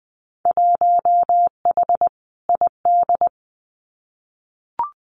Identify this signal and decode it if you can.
Morse code, heard as a keyed tone, about 20 wpm, 700 Hz. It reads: 1H ID
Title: Winter Field Day Exchanges 20wpm